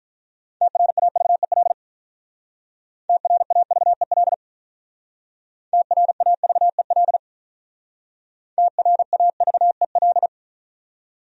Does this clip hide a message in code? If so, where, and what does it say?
Yes: Morse code from 0.6 to 10.3 s, heard as a keyed tone, about 46 wpm, 700 Hz. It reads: TRAVEL TRAVEL TRAVEL TRAVEL